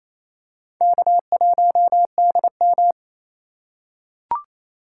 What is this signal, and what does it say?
Morse code, heard as a keyed tone, about 27 wpm, 700 Hz. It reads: K1DM
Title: K1USN SST QSO Practice - 28wpm